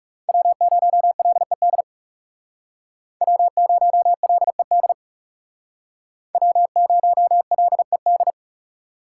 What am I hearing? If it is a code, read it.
Morse code, heard as a keyed tone, about 39 wpm, 700 Hz. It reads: W0LED W0LED W0LED